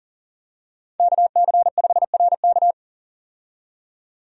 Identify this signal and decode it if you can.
Morse code, heard as a keyed tone, about 40 wpm, 700 Hz. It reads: KC5RK
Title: USA Call Signs - 5 Characters - 40wpm